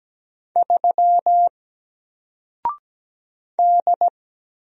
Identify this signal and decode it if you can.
Morse code, heard as a keyed tone, about 17 wpm, 700 Hz. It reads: 3 D
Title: Single Character (Letter:Number:Punctuation:Prosign) Alt 17wpm